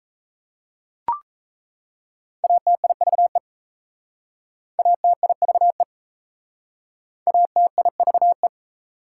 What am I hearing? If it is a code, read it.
Morse code, heard as a keyed tone, about 34 wpm, 690 Hz. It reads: ATIVE ATIVE ATIVE
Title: Speed-Racing Prefixes and Suffixes 28wpm